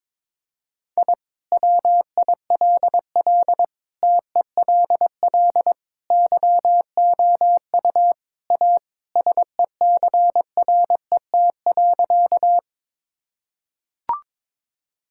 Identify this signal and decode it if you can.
Morse code, heard as a keyed tone, about 22 wpm, 700 Hz. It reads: I WILL TELL YOU A SECRET.
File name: practice / 22wpm / Sentences with 2-Word Phrases - 22wpm